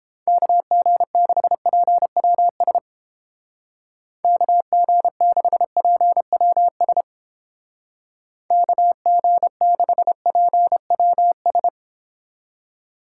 Morse code, heard as a keyed tone, about 30 wpm, 700 Hz. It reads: KG6PWS KG6PWS KG6PWS